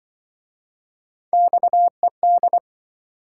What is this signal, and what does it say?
Morse code, heard as a keyed tone, about 26 wpm, 700 Hz. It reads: XED